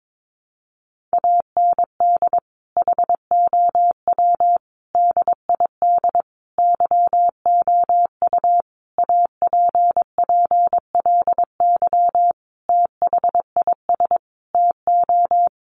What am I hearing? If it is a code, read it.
Morse code, heard as a keyed tone, about 22 wpm, 700 Hz. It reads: AND HOW DID YOU APPLY THIS TO
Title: Sentences from Top 1200 Words 22wpm